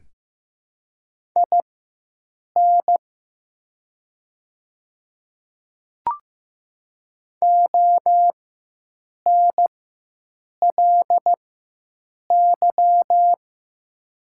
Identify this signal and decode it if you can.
Morse code, heard as a keyed tone, about 15 wpm, spaced as for 7 wpm, 700 Hz. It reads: IN ONLY
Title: 4x Character Spacing - Top 200 Words - 15wpm